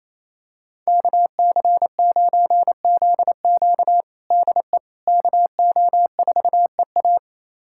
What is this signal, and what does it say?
Morse code, heard as a keyed tone, about 28 wpm, 700 Hz. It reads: KC9ZQ DE KO4EA
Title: USA Callsign DE Callsign - 5 Characters - 28wpm